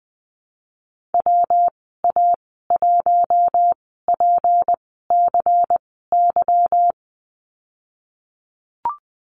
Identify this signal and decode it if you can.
Morse code, heard as a keyed tone, about 20 wpm, spaced as for 15 wpm, 700 Hz. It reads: WA1PCY
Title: 154 Lesson 25g - Letter p - Call Signs 20wpm